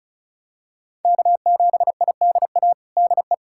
Morse code, heard as a keyed tone, about 34 wpm, 700 Hz. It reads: K7IDA DE
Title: USA Callsign DE Callsign - 5 Characters - 35wpm